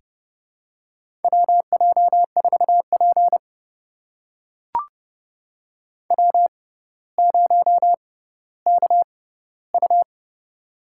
Morse code, heard as a keyed tone, about 30 wpm, spaced as for 15 wpm, 700 Hz. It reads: WJ4P W0KU
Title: USA Call Signs - 6x Character Spacing - 30wpm